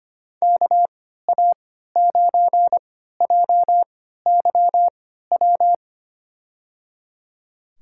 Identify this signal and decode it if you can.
Morse code, heard as a keyed tone, about 25 wpm, spaced as for 14 wpm, 700 Hz. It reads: KA9JYW
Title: USA Call Signs - 3x Character Spacing - 25wpm